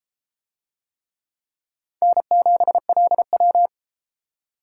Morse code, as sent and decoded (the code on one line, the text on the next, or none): -. --... .-.. .--
N7LW